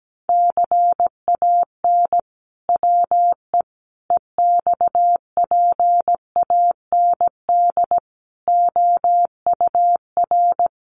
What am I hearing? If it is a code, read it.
Morse code, heard as a keyed tone, about 17 wpm, 700 Hz. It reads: CAN WE EXPAND OUR
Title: Sentences from Top 1400 Words 17wpm